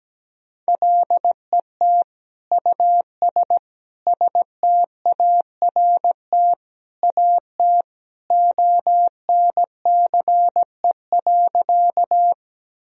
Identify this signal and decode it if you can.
Morse code, heard as a keyed tone, about 17 wpm, 700 Hz. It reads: LET US START AT ONCE.
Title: Sentences from Top 1000 Words - Review Part 1 - 17wpm